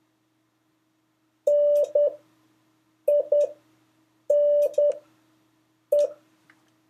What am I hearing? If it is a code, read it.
Morse code, heard as a keyed tone, about 10 wpm, spaced as for 6 wpm, 580 Hz. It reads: NINE
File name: 3of5-words-2